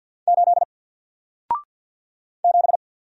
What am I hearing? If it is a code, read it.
Morse code, heard as a keyed tone, about 48 wpm, 700 Hz. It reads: 8 7
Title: Single Character (Letter:Number:Punctuation:Prosign) Alt 50wpm